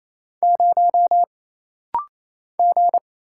Morse code, heard as a keyed tone, about 28 wpm, 700 Hz. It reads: 0 G
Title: Single Letter-Number Alt - Rapid-Fire 28wpm